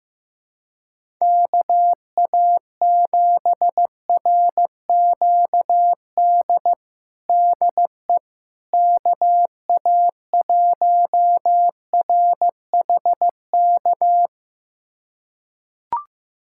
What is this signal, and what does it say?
Morse code, heard as a keyed tone, about 16 wpm, 700 Hz. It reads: KA7RQD DE KA1RHK
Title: USA Callsign DE Callsign - 6 Characters - 15wpm